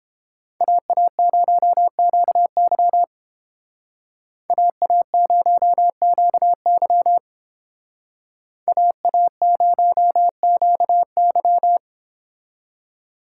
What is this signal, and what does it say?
Morse code, heard as a keyed tone, about 28 wpm, 700 Hz. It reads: AA0QY AA0QY AA0QY